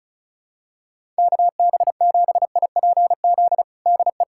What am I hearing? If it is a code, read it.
Morse code, heard as a keyed tone, about 36 wpm, 700 Hz. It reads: KB7IPZ DE
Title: USA Callsign DE Callsign 35wpm